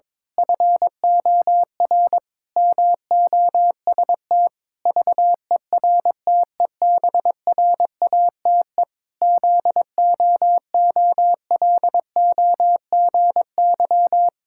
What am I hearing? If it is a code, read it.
Morse code, heard as a keyed tone, about 22 wpm, 700 Hz. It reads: FOR MOST VERTEBRATE ZOOLOGY
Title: Pangram Challenge v2 - 22wpm